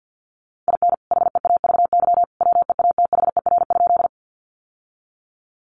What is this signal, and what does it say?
Morse code, heard as a keyed tone, about 50 wpm, 700 Hz. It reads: IN HEAVY WEATHER?
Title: Sentences from Top 1100 Words 50wpm